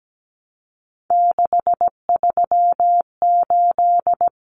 Morse code, heard as a keyed tone, about 17 wpm, 700 Hz. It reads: 638
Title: Numbers - 3-Digit 17wpm